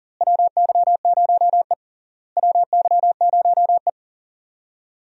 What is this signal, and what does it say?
Morse code, heard as a keyed tone, about 40 wpm, 700 Hz. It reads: WY0E WY0E
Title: USA Call Signs - 4 Characters - Repeated 3x - 40wpm